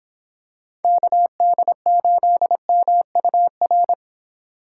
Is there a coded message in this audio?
Yes: Morse code from 0.8 to 3.9 s, heard as a keyed tone, about 26 wpm, 700 Hz. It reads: KD8MUR